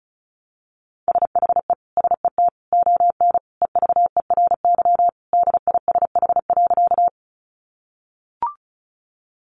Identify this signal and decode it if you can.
Morse code, heard as a keyed tone, about 34 wpm, 700 Hz. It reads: SHE SET ON EVERY DISH.
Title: Sentences from Top 1300 Words 35wpm